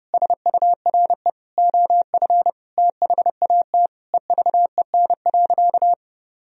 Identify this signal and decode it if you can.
Morse code, heard as a keyed tone, about 32 wpm, 700 Hz. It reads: SURE OF THAT EVEN.
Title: Sentences from Top 1000 Words - Review Part 1 - 30wpm